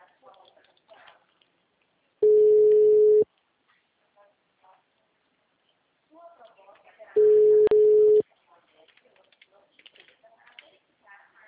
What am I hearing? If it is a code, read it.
Ringback tone (Europe, 425 Hz), with speech.